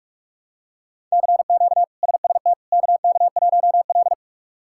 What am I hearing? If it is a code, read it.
Morse code, heard as a keyed tone, about 46 wpm, 700 Hz. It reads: CQ SST KK1L
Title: K1USN SST QSO Practice - 45wpm